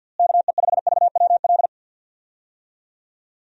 Morse code, heard as a keyed tone, about 47 wpm, 700 Hz. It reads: KE5VWL
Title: USA Call Signs - 6 Characters - Repeated 2x - 50wpm